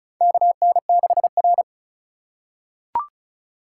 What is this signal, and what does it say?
Morse code, heard as a keyed tone, about 36 wpm, 700 Hz. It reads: KN6R
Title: USA Call Signs - 4 Characters - 4x Character Spacing - 35wpm